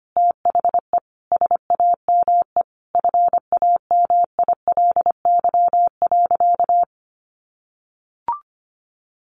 Morse code, heard as a keyed tone, about 25 wpm, 700 Hz. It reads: THE SAME FAMILY.
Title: Sentences from Top 1500 Words 25wpm